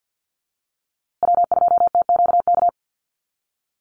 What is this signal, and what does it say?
Morse code, heard as a keyed tone, about 50 wpm, 700 Hz. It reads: W2TQK
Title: USA Call Signs Repeated 3x 50wpm